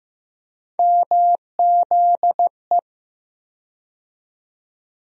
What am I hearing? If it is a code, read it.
Morse code, heard as a keyed tone, about 15 wpm, 700 Hz. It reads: MZE
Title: USA Call Sign Postfixes - 3 Letters 15wpm